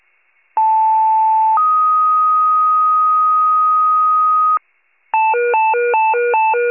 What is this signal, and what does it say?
Two-tone page, heard as a one-second tone, then a three-second tone, about 880 then 1250 Hz. It is heard from 0.6 to 4.6 s.